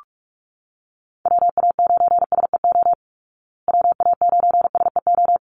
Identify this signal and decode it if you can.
Morse code, heard as a keyed tone, about 44 wpm, 700 Hz. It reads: WA9SEO WA9SEO
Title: Call Signs Repeated 3x - 45wpm